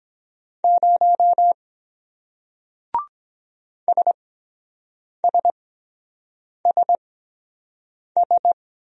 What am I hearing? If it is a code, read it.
Morse code, heard as a keyed tone, about 23 wpm, spaced as for 7 wpm, 700 Hz. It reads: 0 SSSS